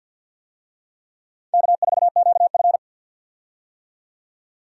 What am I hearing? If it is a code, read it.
Morse code, heard as a keyed tone, about 53 wpm, 700 Hz. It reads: K4QF
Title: USA Call Signs - 4 Characters - 50wpm